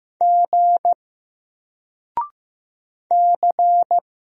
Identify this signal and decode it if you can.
Morse code, heard as a keyed tone, about 15 wpm, 700 Hz. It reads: G C
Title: Single Letter-Number 15wpm